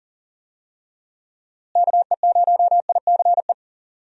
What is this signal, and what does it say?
Morse code, heard as a keyed tone, about 39 wpm, 700 Hz. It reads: KE0ICE
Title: USA Call Signs - 6 Characters - Repeated 3x - 40wpm